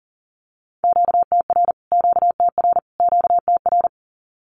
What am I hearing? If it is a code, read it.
Morse code, heard as a keyed tone, about 40 wpm, 700 Hz. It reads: QTR QTR QTR
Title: Q-Codes Repeated 3x 40wpm